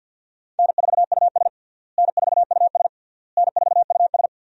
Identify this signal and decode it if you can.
Morse code, heard as a keyed tone, about 53 wpm, 700 Hz. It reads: N4US N4US N4US
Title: USA Call Signs - 4 Characters - Repeated 3x - 50wpm